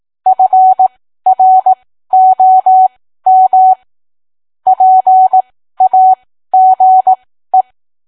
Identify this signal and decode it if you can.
Morse code, heard as a keyed tone, about 18 wpm, spaced as for 13 wpm, 750 Hz. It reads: FROM PAGE